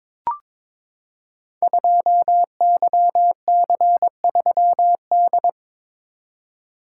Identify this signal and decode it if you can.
Morse code, heard as a keyed tone, about 23 wpm, 700 Hz. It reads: 2YC3D
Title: Code Groups - 5 Letter:Numbers - 22wpm